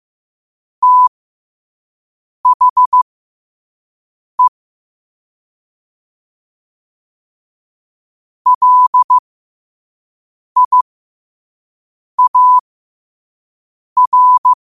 Morse code, heard as a keyed tone, about 15 wpm, spaced as for 5 wpm, 1000 Hz. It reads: THE LIAR
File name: qotdfiles / 5wpm